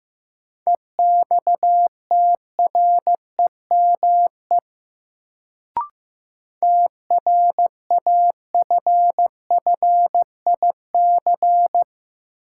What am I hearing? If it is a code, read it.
Morse code, heard as a keyed tone, about 15 wpm, 700 Hz. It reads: EXTREME TRAFFIC
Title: Sentences from Top 1300 Words - New Word Drill Alt - 15wpm